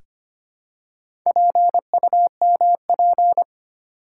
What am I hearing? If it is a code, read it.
Morse code, heard as a keyed tone, about 25 wpm, 700 Hz. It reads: PUMP
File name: practice / 25wpm / Easy Words with Hints - 25wpm